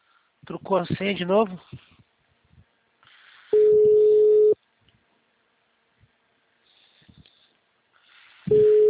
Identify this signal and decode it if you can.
Ringback tone (Europe, 425 Hz), with speech.